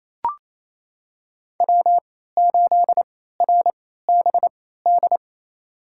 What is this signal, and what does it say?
Morse code, heard as a keyed tone, about 28 wpm, spaced as for 16 wpm, 700 Hz. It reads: W8RBD
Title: USA Call Signs - 3x Character Spacing - 28wpm